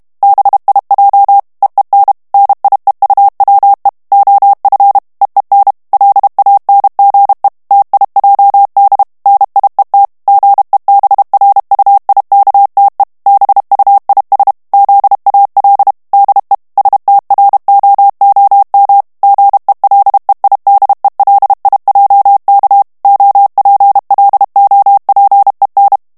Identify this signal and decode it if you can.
Morse code, heard as a keyed tone, about 32 wpm, 800 Hz. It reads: BIJ EEN NIEUWE OF EEN LANGE TIJD NIET GEBRUIKTE BUIS ZAL DE STROOM GELEIDELIJK OPLOPEN
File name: HOLLANDAIS texte clair 2 vitesse 30 mots minute